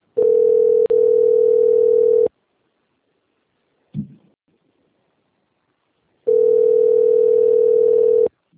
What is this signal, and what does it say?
Ringback tone (North America), 440 + 480 Hz.
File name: out- - - - - .8881